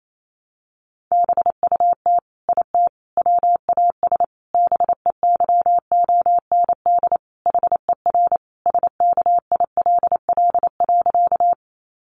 Morse code, heard as a keyed tone, about 28 wpm, 700 Hz. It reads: BUT IT WAS BEYOND HER SKILL.
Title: Sentences from Top 1200 Words 28wpm